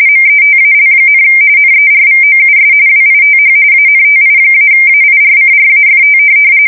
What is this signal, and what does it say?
RTTY, about 45.45 baud. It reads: METAL, NERVOUS WITH ANTI-AIRCRAFT
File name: fsk170 / USS-New-Jersey